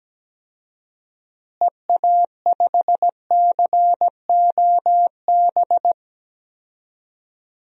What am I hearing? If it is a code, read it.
Morse code, heard as a keyed tone, about 17 wpm, 700 Hz. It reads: EA5COB
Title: DXCC-100 Call Signs - 6 Characters - Repeated 3x - 17wpm